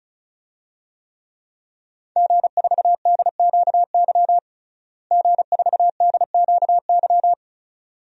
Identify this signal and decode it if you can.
Morse code, heard as a keyed tone, about 34 wpm, 700 Hz. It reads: G4DQY G4DQY
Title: DXCC-100 Call Signs - 5 Characters - Repeated 2x - 35wpm